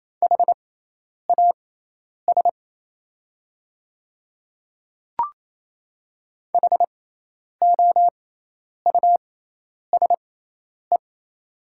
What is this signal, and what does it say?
Morse code, heard as a keyed tone, about 28 wpm, spaced as for 10 wpm, 700 Hz. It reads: HAS HOUSE
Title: 6x Character Spacing - Top 200 Words - 28wpm